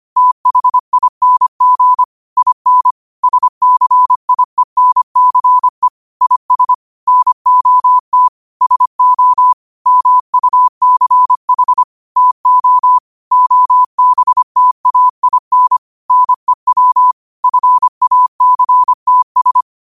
Morse code, heard as a keyed tone, about 25 wpm, 1000 Hz. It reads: THING IN SCIENCE IS NOT SO MUCH TO OBTAIN NEW FACTS
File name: qotdfiles / 25wpm